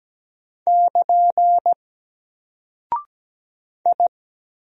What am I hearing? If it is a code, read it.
Morse code, heard as a keyed tone, about 17 wpm, 700 Hz. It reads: <KN> I
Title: Single Character (Letter:Number:Punctuation:Prosign) 17wpm